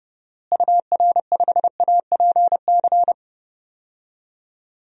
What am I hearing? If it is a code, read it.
Morse code, heard as a keyed tone, about 32 wpm, 700 Hz. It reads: UR5APC
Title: DXCC-100 Call Signs - 6 Characters - Repeated 2x - 30wpm